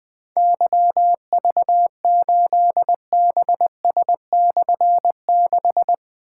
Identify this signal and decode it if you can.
Morse code, heard as a keyed tone, about 20 wpm, 700 Hz. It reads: YV8BS/6
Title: Difficult Call Signs 20wpm